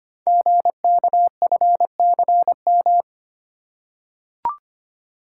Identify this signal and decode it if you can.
Morse code, heard as a keyed tone, about 25 wpm, 690 Hz. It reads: GKFCM
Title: Code Groups - 5 Letters - 25wpm